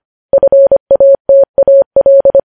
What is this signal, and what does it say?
Morse code, heard as a keyed tone, about 25 wpm, 550 Hz. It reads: FATAL
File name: podcast / News headlines-1x-25x25